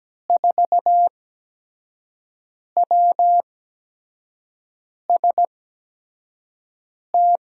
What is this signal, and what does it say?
Morse code, heard as a keyed tone, about 17 wpm, spaced as for 5 wpm, 700 Hz. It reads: 4WST
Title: USA Call Signs - 6 Characters - 8x Character Spacing - 17wpm